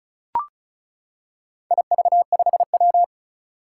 Morse code, heard as a keyed tone, about 36 wpm, 690 Hz. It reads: IV5W
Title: Four Challenging Letter-Number Combinations 35wpm